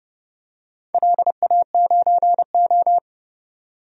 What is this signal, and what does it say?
Morse code, heard as a keyed tone, about 30 wpm, 700 Hz. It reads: LA9O
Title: DXCC-100 Call Signs - 4 Characters - 30wpm